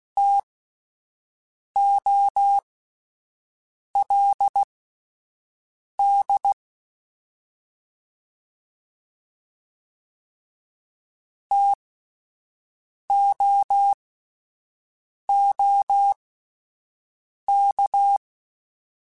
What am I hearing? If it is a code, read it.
Morse code, heard as a keyed tone, about 16 wpm, spaced as for 5 wpm, 800 Hz. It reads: TOLD TOOK